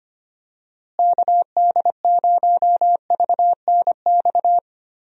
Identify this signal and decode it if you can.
Morse code, heard as a keyed tone, about 25 wpm, 700 Hz. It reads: KD0VNX
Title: USA Call Signs - 6 Characters - Repeated 2x - 25wpm